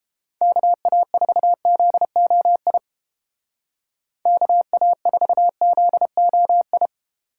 Morse code, heard as a keyed tone, about 28 wpm, 700 Hz. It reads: KA4ZOI KA4ZOI